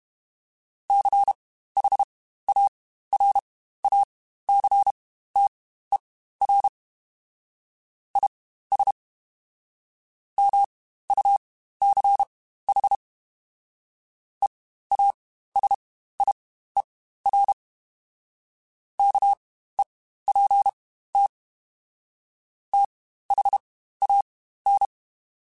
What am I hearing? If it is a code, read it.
Morse code, heard as a keyed tone, about 32 wpm, spaced as for 15 wpm, 800 Hz. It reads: CHARACTER IS MUCH EASIER KEPT THAN
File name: Quotes Spaced / Quotes Spaced-mp3s / Character sp @30